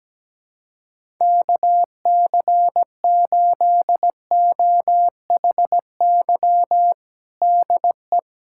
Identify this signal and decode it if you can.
Morse code, heard as a keyed tone, about 17 wpm, 700 Hz. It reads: KC8OHY DE
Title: USA Callsign DE Callsign 17wpm